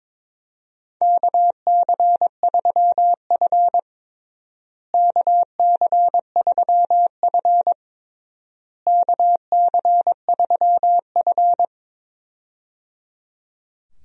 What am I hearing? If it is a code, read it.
Morse code, heard as a keyed tone, about 21 wpm, 700 Hz. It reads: KC3F KC3F KC3F
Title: USA Call Signs - 4 Characters - Repeated 3x - 22wpm